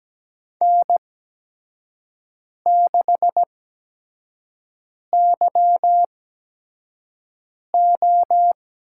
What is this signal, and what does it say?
Morse code, heard as a keyed tone, about 17 wpm, 700 Hz. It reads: N 6 Y O
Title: USA Call Signs - 8x Character Spacing - 17wpm